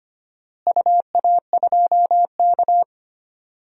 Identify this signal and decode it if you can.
Morse code, heard as a keyed tone, about 25 wpm, 700 Hz. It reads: UA2K
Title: DXCC-100 Call Signs - 4 Characters - Repeated 3x - 25wpm